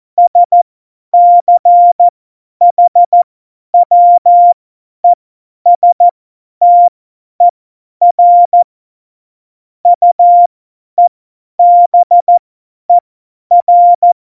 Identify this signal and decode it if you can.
Morse code, heard as a keyed tone, about 14 wpm, spaced as for 10 wpm, 700 Hz. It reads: SCHWESTER UEBER
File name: Morsekurs / Witze / Witz 15. September 2020, 50 F70